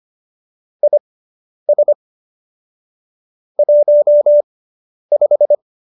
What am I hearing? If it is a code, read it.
Morse code, heard as a keyed tone, about 25 wpm, spaced as for 10 wpm, 590 Hz. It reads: IS 15